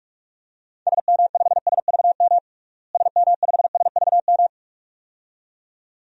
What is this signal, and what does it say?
Morse code, heard as a keyed tone, about 46 wpm, 700 Hz. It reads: SM5SVM SM5SVM
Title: DXCC-100 Call Signs - 6 Characters - Repeated 2x - 45wpm